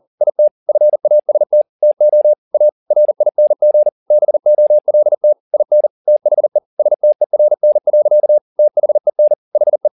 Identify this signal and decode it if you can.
Morse code, heard as a keyed tone, about 40 wpm, 600 Hz. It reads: IT FAST TO A RING BOLT IN THE STERN. THEN HE